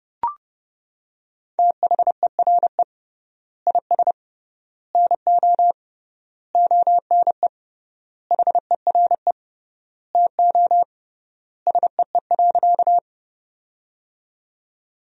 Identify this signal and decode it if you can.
Morse code, heard as a keyed tone, about 30 wpm, 700 Hz. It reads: THERE IS NO ONE HERE TO SEE.
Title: 3x Word-Spacing - Sentences from Top 200 Words - 30wpm